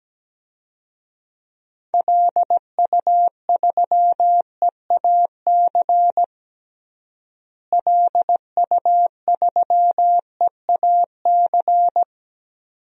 Morse code, heard as a keyed tone, about 17 wpm, 700 Hz. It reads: LU3EAC LU3EAC
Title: DXCC-100 Call Signs - 6 Characters - Repeated 2x - 17wpm